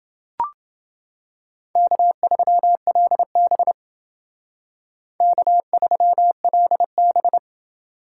Morse code, heard as a keyed tone, about 28 wpm, 700 Hz. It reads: K3LB K3LB